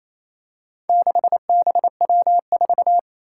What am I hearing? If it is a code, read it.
Morse code, heard as a keyed tone, about 28 wpm, 700 Hz. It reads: 6BW4
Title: Four Challenging Letter-Number Combinations 28wpm